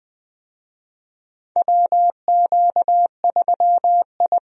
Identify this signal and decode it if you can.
Morse code, heard as a keyed tone, about 20 wpm, 700 Hz. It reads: WQ3I
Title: USA Call Signs - 4 Characters - Repeated 2x - 20wpm